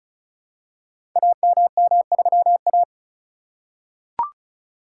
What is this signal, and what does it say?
Morse code, heard as a keyed tone, about 36 wpm, 700 Hz. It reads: AMM3A
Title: Code Groups - 5 Letter:Numbers - 35wpm